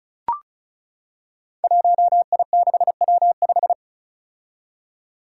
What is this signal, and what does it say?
Morse code, heard as a keyed tone, about 36 wpm, 700 Hz. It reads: 1I6W5
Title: Five Challenging Letter-Number Combinations 35wpm